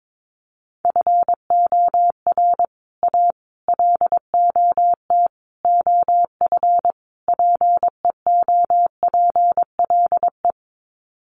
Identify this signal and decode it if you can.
Morse code, heard as a keyed tone, about 22 wpm, 700 Hz. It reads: FOR A LOT OF PEOPLE
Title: Sets of 5 Words Encore 22wpm